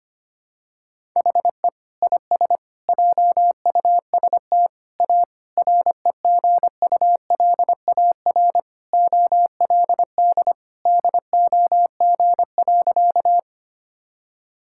Morse code, heard as a keyed tone, about 25 wpm, 700 Hz. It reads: HE IS JUST A REGULAR OLD DOG.
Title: Sentences from Top 1000 Words - Review Part 6 - 25wpm